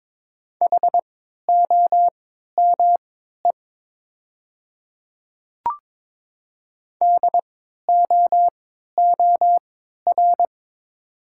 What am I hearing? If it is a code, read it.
Morse code, heard as a keyed tone, about 22 wpm, spaced as for 12 wpm, 700 Hz. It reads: HOME DOOR
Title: 3x Character Spacing - Top 300 Words - 22wpm